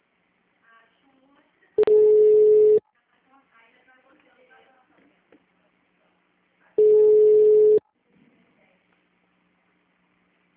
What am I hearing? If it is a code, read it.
Ringback tone (Europe, 425 Hz), with speech.